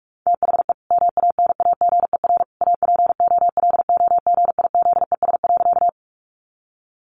Morse code, heard as a keyed tone, about 44 wpm, 700 Hz. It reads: THE MANAGER APOLOGIZES.
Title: Additional Easy Phrases 45wpm